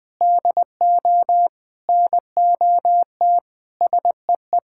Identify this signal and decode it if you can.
Morse code, heard as a keyed tone, about 20 wpm, 700 Hz. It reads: DO NOT SEE
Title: Sentences with 4-Word Phrases - Spoken Beginning - 20wpm